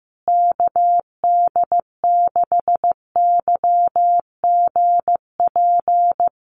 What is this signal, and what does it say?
Morse code, heard as a keyed tone, about 15 wpm, 700 Hz. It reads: KD6YGP
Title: USA Call Signs Repeated 3x 15wpm